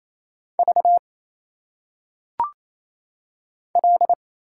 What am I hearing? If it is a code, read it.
Morse code, heard as a keyed tone, about 27 wpm, 700 Hz. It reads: V L
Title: Single Letters - Phonetic Pronunciation 28wpm